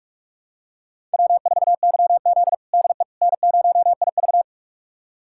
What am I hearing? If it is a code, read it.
Morse code, heard as a keyed tone, about 46 wpm, 700 Hz. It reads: W4YZ DE N0IV
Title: USA Callsign DE Callsign - 4 Characters - 45wpm